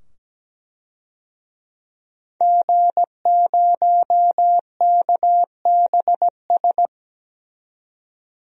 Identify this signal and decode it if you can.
Morse code, heard as a keyed tone, about 17 wpm, 700 Hz. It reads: G0KBS
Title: DXCC-100 Call Signs - 5 Characters - 17wpm